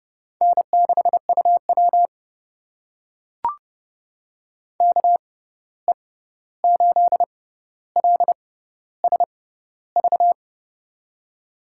Morse code, heard as a keyed tone, about 30 wpm, spaced as for 14 wpm, 700 Hz. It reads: N6UW KE8LSV
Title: USA Call Signs - 6x Character Spacing - 30wpm